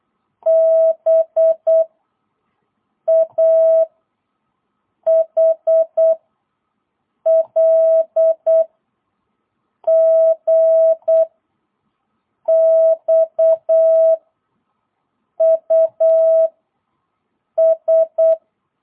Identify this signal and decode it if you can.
Morse code, heard as a keyed tone, about 8 wpm, 660 Hz. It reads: B A H L G X U S
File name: jh8cbh / 2022cw